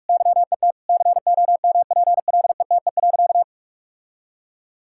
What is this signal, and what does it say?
Morse code, heard as a keyed tone, about 41 wpm, 700 Hz. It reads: YET COMPLETE.
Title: Sentences from Top 1000 Words - Review Part 6 - 45wpm